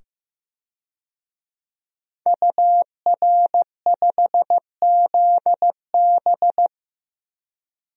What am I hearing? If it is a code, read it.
Morse code, heard as a keyed tone, about 15 wpm, 700 Hz. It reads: UR5ZB
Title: DXCC-100 Call Signs - 5 Characters - 15wpm